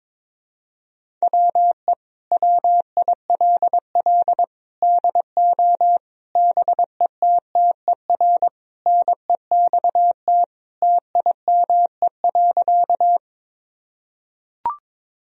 Morse code, heard as a keyed tone, about 22 wpm, 700 Hz. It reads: WE WILL DO BETTER NEXT TIME.
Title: Sentences from Top 1000 Words - Review Part 1 - 22wpm